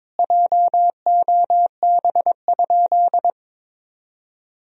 Morse code, heard as a keyed tone, about 21 wpm, 700 Hz. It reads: JOB?
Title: Sentences from Top 1000 Words - Review Part 3 - 22wpm